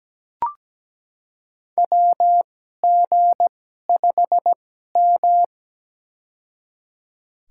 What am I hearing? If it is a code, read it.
Morse code, heard as a keyed tone, about 17 wpm, spaced as for 12 wpm, 700 Hz. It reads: WG5M
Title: USA Call Signs - 4 Characters - 2x Character Spacing - 17wpm